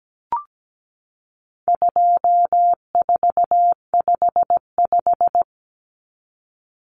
Morse code, heard as a keyed tone, about 17 wpm, 700 Hz. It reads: 2455
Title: Common Cut Numbers - 4-Digit 17wpm